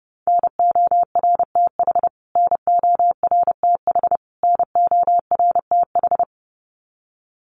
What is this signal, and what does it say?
Morse code, heard as a keyed tone, about 30 wpm, 700 Hz. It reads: NORTH NORTH NORTH
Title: Top 300 Words Repeated 30wpm